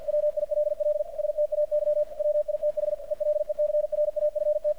Morse code, heard as a keyed tone, about 50 wpm, 600 Hz, with radio noise. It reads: WILL STAY WITH ELECRAFT
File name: archived / CRSnet